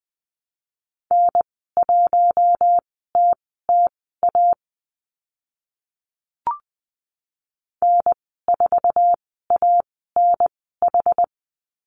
Morse code, heard as a keyed tone, about 20 wpm, spaced as for 15 wpm, 700 Hz. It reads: N1TTA N4ANH
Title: 076 Lesson 13g - Letter L - Call Signs 20wpm